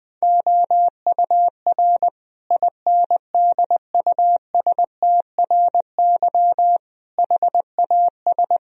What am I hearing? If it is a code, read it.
Morse code, heard as a keyed tone, about 20 wpm, 700 Hz. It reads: OUR INDUSTRY HAS
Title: Sentences from Top 1000 Words - Review Part 4 - 20wpm